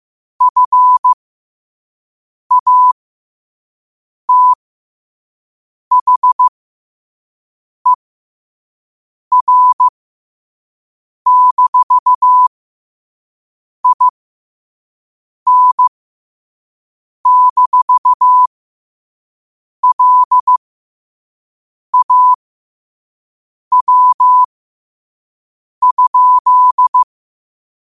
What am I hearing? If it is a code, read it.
Morse code, heard as a keyed tone, about 15 wpm, spaced as for 5 wpm, 1000 Hz. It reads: FATHER-IN-LAW?
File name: qotdfiles / 5wpm